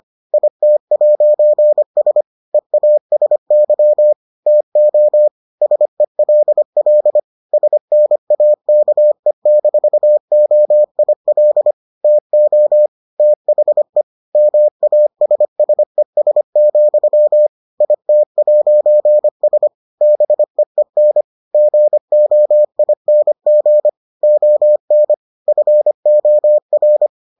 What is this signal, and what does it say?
Morse code, heard as a keyed tone, about 25 wpm, 600 Hz. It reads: IT'S EASY TO SELL SNAKE-OIL TO THE MASSES, IT'S BEEN GOING ON FOR